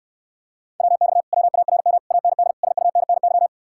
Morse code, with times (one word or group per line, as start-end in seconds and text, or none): 0.8-1.2 WX
1.3-2.0 WARM
2.1-2.5 AND
2.6-3.5 SUNNY